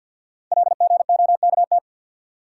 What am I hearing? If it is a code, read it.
Morse code, heard as a keyed tone, about 47 wpm, 700 Hz. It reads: LGOKT
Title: Code Groups - 5 Letters - 50wpm